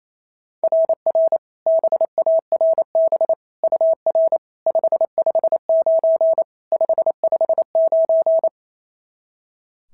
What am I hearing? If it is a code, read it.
Morse code, heard as a keyed tone, about 25 wpm, 650 Hz. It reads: RR BARB UR 559 559
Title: SOTA QSO Practice 28wpm